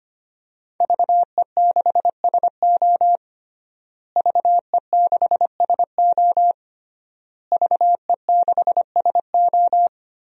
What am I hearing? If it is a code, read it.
Morse code, heard as a keyed tone, about 25 wpm, 700 Hz. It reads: VE6SO VE6SO VE6SO
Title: DXCC-100 Call Signs - 5 Characters - Repeated 3x - 25wpm